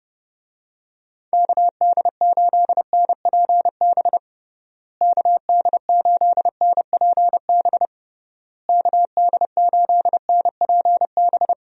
Morse code, heard as a keyed tone, about 30 wpm, 700 Hz. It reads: KD8NPB KD8NPB KD8NPB
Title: USA Call Signs - 6 Characters - Repeated 3x - 30wpm